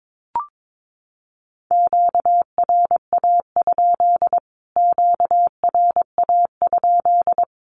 Morse code, heard as a keyed tone, about 22 wpm, 700 Hz. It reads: QRA? QRA?
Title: Q-Codes Repeated 3x 22wpm